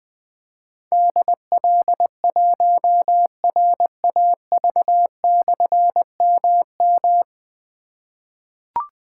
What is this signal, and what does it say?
Morse code, heard as a keyed tone, about 20 wpm, 700 Hz. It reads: DL1RAV/MM
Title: Difficult Call Signs 20wpm